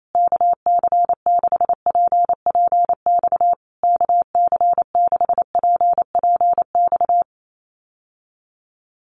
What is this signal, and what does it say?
Morse code, heard as a keyed tone, about 28 wpm, 700 Hz. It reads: KC6PPX KC6PPX
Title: USA Call Signs Repeated 3x 28wpm